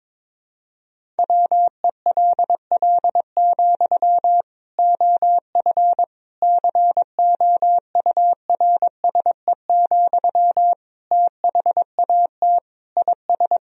Morse code, heard as a keyed tone, about 22 wpm, 700 Hz. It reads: WELL, OF COURSE, THAT IS
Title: Sentences from Top 1000 Words - Review Part 1 - 22wpm